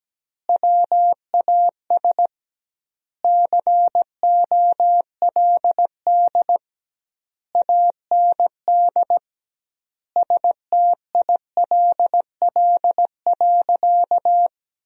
Morse code, heard as a keyed tone, about 17 wpm, 700 Hz. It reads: WAS COLD AND STILL.
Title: 2x Word-Spacing - Sentences from Top 400 Words - 17wpm